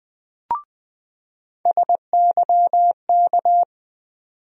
Morse code, heard as a keyed tone, about 20 wpm, 700 Hz. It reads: SYK
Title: USA Call Sign Postfixes - 3 Letters 20wpm